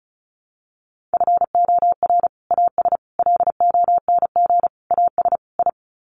Morse code, heard as a keyed tone, about 34 wpm, 700 Hz. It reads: FOR AS LONG AS I
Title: Sets of 5 Words Encore2 35wpm